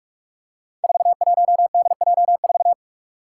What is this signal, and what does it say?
Morse code, heard as a keyed tone, about 46 wpm, 700 Hz. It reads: 41DJ4
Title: Five Challenging Letter-Number Combinations 45wpm